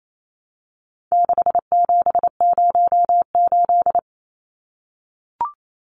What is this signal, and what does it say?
Morse code, heard as a keyed tone, about 28 wpm, 700 Hz. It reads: 6708
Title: Numbers - 4-Digit 28wpm